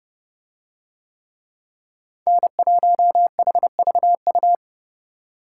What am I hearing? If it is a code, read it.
Morse code, heard as a keyed tone, about 28 wpm, 700 Hz. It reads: N1HVU